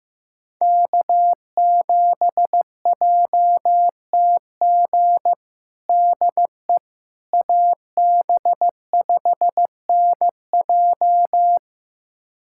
Morse code, heard as a keyed tone, about 15 wpm, 700 Hz. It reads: K7JTG DE AB5NJ
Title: USA Callsign DE Callsign - 5 Characters - 15wpm